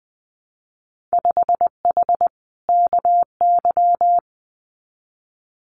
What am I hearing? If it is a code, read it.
Morse code, heard as a keyed tone, about 19 wpm, 700 Hz. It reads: 5H KY
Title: Winter Field Day Exchanges Encore 20wpm